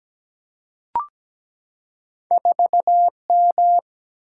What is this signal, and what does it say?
Morse code, heard as a keyed tone, about 17 wpm, 700 Hz. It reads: 4M
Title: Single-Letter Single-Number 17wpm